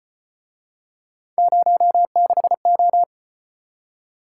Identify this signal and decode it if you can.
Morse code, heard as a keyed tone, about 35 wpm, 700 Hz. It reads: 06O